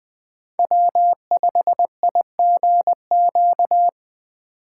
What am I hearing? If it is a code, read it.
Morse code, heard as a keyed tone, about 20 wpm, 700 Hz. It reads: W5IGQ
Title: USA Call Signs - 5 Characters - 3x Character Spacing - 20wpm